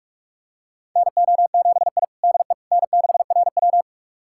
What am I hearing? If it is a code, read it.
Morse code, heard as a keyed tone, about 41 wpm, 690 Hz. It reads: NO7I DE N6RW
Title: USA Callsign DE Callsign - 4 Characters - 45wpm